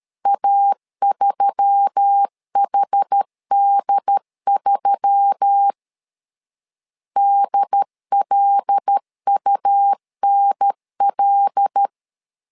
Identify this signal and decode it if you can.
Morse code, heard as a keyed tone, about 13 wpm, 800 Hz. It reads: A3HD3 DLUNL
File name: LEZIONI / 5 LBVC / 5 L 12wpm 1913